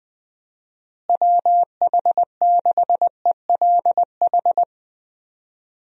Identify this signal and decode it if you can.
Morse code, heard as a keyed tone, about 20 wpm, 700 Hz. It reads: WH6ELH
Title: USA Call Signs - 6 Characters - Repeated 2x - 20wpm